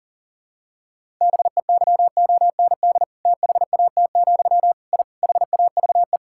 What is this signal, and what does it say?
Morse code, heard as a keyed tone, about 40 wpm, 700 Hz. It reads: BEYOND THAT, I HAVE
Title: Sentences from Top 1000 Words - Review Part 6 - 40wpm